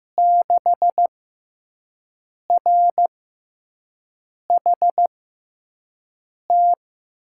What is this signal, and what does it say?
Morse code, heard as a keyed tone, about 15 wpm, spaced as for 5 wpm, 700 Hz. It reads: 6RHT
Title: USA Call Signs - 5 Characters - 6x Character Spacing - 15wpm